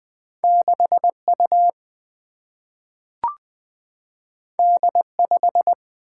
Morse code, heard as a keyed tone, about 20 wpm, 700 Hz. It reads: 6 U D 5
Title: Single-Letter Single-Number 20wpm